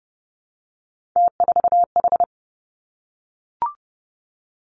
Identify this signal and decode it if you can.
Morse code, heard as a keyed tone, about 30 wpm, 700 Hz. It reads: T4H
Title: Three Challenging Letter-Number Combinations 30wpm